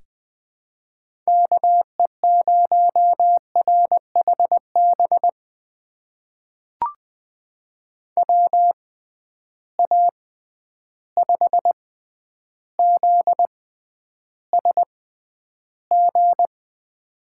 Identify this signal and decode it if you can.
Morse code, heard as a keyed tone, about 20 wpm, spaced as for 10 wpm, 700 Hz. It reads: KE0RHB WA5ZSG
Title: USA Call Signs - 6 Characters - 6x Character Spacing - 20wpm